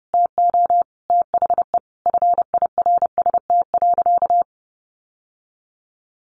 Morse code, heard as a keyed tone, about 30 wpm, 700 Hz. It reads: TO THE FIRST.
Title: Sentences from Top 800 Words 30wpm